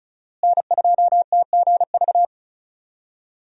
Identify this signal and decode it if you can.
Morse code, heard as a keyed tone, about 34 wpm, 700 Hz. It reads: N2TGV
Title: USA Call Signs - 5 Characters - Repeated 3x - 35wpm